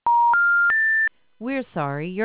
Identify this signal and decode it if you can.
Special information tone, with speech.